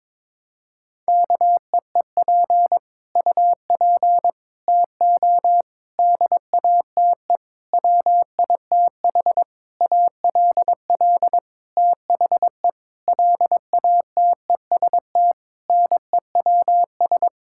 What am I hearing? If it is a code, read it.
Morse code, heard as a keyed tone, about 22 wpm, 700 Hz. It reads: KEEP UP TO DATE WITH ALL THE LATEST NEWS
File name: Sentences with 9-Word Phrases - 22wpm